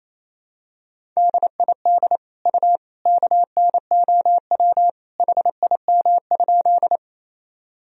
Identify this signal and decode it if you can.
Morse code, heard as a keyed tone, about 28 wpm, 700 Hz. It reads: DID U KNOW HIM?
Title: Sentences from Top 100 Words and Abbreviations 28wpm